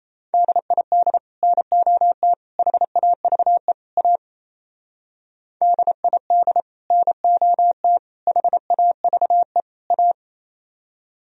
Morse code, heard as a keyed tone, about 35 wpm, 700 Hz. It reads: DID NOT HAVE A DID NOT HAVE A
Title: Speed-Racing - Sets of 4 Words 28wpm